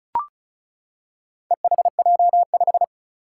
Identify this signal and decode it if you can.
Morse code, heard as a keyed tone, about 36 wpm, 700 Hz. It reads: EHJ5
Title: Four Challenging Letter-Number Combinations 35wpm